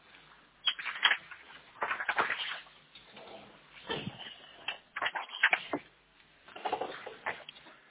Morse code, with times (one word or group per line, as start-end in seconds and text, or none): none